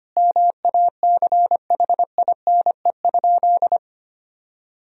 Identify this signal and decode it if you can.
Morse code, heard as a keyed tone, about 25 wpm, 700 Hz. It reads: MACHINE?
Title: Sentences from Top 1000 Words - Review Part 2 - 25wpm